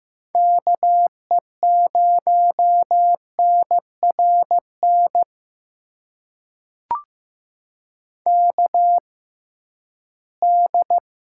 Morse code, heard as a keyed tone, about 15 wpm, spaced as for 11 wpm, 700 Hz. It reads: KE0NRN KD
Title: USA Call Signs - 6x Character Spacing - 15wpm